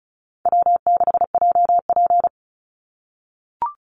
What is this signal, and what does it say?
Morse code, heard as a keyed tone, about 36 wpm, 700 Hz. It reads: W6JP
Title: NA SOTA Chaser Callsigns 35wpm